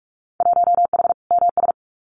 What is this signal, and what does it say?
Morse code, heard as a keyed tone, about 44 wpm, 700 Hz. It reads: 1H MS
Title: Winter Field Day Exchanges 45wpm